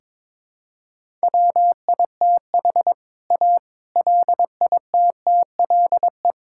Morse code, heard as a keyed tone, about 22 wpm, 700 Hz. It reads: WITH A LITTLE
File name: Sentences with 7-Word Phrases - 22wpm